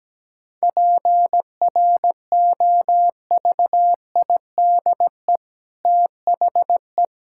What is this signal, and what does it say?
Morse code, heard as a keyed tone, about 17 wpm, 700 Hz. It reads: PROVIDE THE
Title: Sentences from Top 1000 Words - Review Part 5 - 17wpm